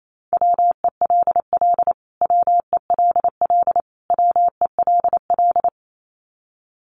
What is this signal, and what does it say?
Morse code, heard as a keyed tone, about 28 wpm, 700 Hz. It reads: WELL WELL WELL
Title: Top 200 Words Repeated 28wpm